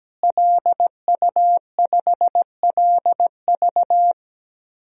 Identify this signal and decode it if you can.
Morse code, heard as a keyed tone, about 17 wpm, 700 Hz. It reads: LU5LV
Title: DXCC-100 Call Signs - 5 Characters - Repeated 3x - 17wpm